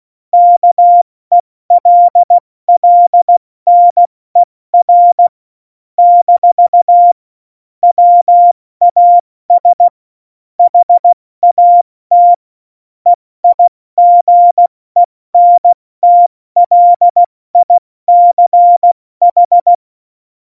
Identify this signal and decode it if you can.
Morse code, heard as a keyed tone, about 16 wpm, 700 Hz. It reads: KELLNER - WAS HAT EIGENTLICH
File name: Morsekurs / Witze / Witz 10. September 2019, 70 F80